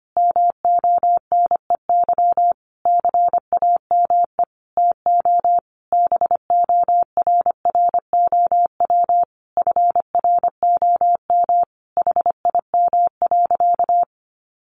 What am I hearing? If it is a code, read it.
Morse code, heard as a keyed tone, about 25 wpm, 700 Hz. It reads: MONEY CAME TO BORROW FROM HIM.
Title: Sentences from Top 1500 Words 25wpm